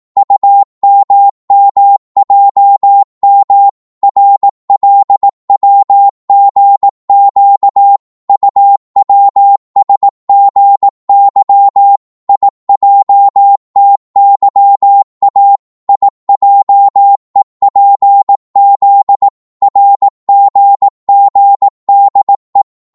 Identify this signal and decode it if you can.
Morse code, heard as a keyed tone, about 18 wpm, 800 Hz. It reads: UMMJM RLWGQ UWSGY IJTYA IJEPZ RGGDE
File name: Morse / ltr-18wpm